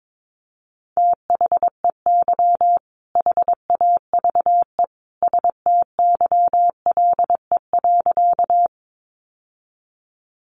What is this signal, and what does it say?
Morse code, heard as a keyed tone, about 22 wpm, 700 Hz. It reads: THEY HAVE STYLE.
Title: Sentences from Top 800 Words 22wpm